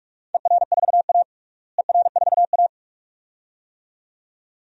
Morse code, heard as a keyed tone, about 41 wpm, 700 Hz. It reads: ER4A ER4A
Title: DXCC-100 Call Signs - 4 Characters - Repeated 2x - 45wpm